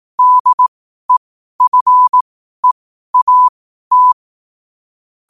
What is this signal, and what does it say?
Morse code, heard as a keyed tone, about 18 wpm, spaced as for 12 wpm, 1000 Hz. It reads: DEFEAT
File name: qotdfiles / 12wpm